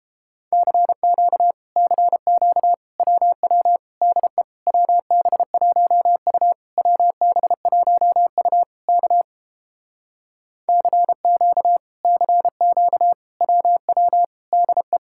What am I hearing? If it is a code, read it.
Morse code, heard as a keyed tone, about 29 wpm, 700 Hz. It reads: CQ CQ WW DE WB1U WB1U K CQ CQ WW DE